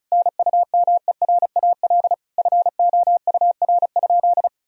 Morse code, heard as a keyed tone, about 34 wpm, 700 Hz. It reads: NUMERAL FOUR?
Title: Sentences from Top 1000 Words - Review Part 2 - 35wpm